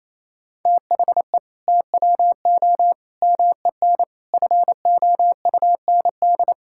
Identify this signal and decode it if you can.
Morse code, heard as a keyed tone, about 28 wpm, 700 Hz. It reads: THE TWO MEN FOUND
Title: Sentences from Top 1000 Words - Review Part 5 - 28wpm